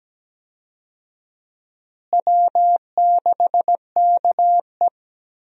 Morse code, heard as a keyed tone, about 17 wpm, 700 Hz. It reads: W6KE
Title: USA Call Signs - 4 Characters - Repeated 3x - 17wpm